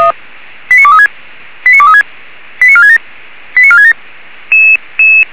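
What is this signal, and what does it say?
Five-tone selective call (ZVEI), heard as quick runs of five tones, tones of about 70 ms. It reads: 89236 89236 89467 89467
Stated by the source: ZVEI five-tone selective call 89373 received on an SDR.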